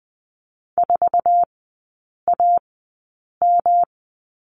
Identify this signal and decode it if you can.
Morse code, heard as a keyed tone, about 20 wpm, spaced as for 8 wpm, 700 Hz. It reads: 4AM
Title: 121 Lesson 20e - Number 3 - Three Characters 20wpm